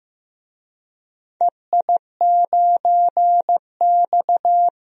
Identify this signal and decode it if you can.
Morse code, heard as a keyed tone, about 15 wpm, 700 Hz. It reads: EI9X
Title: DXCC-100 Call Signs - 4 Characters - 15wpm